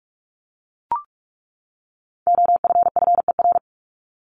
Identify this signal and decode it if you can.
Morse code, heard as a keyed tone, about 44 wpm, 700 Hz. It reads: OFFER
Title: ED Suffix - Top 100 Words with Base Words 45wpm